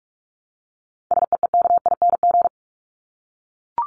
Morse code, heard as a keyed tone, about 44 wpm, 700 Hz. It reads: SEEKING
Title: ING Suffix - Top 100 Words with Base Words 45wpm